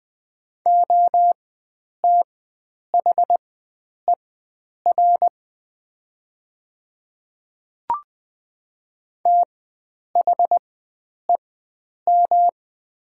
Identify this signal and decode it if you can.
Morse code, heard as a keyed tone, about 20 wpm, spaced as for 9 wpm, 700 Hz. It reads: OTHER THEM
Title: 4x Character Spacing - Top 300 Words - 20wpm